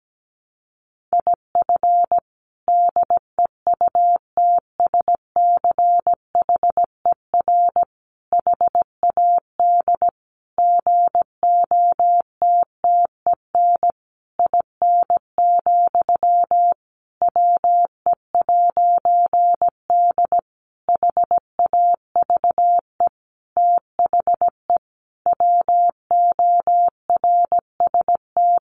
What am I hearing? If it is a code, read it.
Morse code, heard as a keyed tone, about 17 wpm, 700 Hz. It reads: IF DEUTSCHER HAD GOTTEN IN, WE'D HAVE THE WORST
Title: A Sound Of Thunder 17wpm